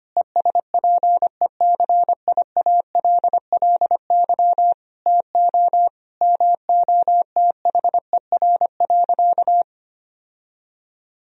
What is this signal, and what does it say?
Morse code, heard as a keyed tone, about 25 wpm, 700 Hz. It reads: ESPECIALLY TO MOTHER.
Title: Sentences from Top 1000 Words - Review Part 4 - 25wpm